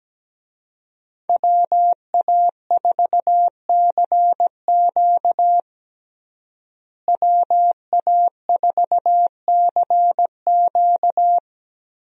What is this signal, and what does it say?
Morse code, heard as a keyed tone, about 17 wpm, 700 Hz. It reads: WA4CQ WA4CQ
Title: USA Call Signs - 5 Characters - Repeated 2x - 17wpm